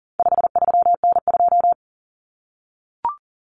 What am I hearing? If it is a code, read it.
Morse code, heard as a keyed tone, about 40 wpm, 700 Hz. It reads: 53N2
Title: Common Cut Numbers - 4-Digit 40wpm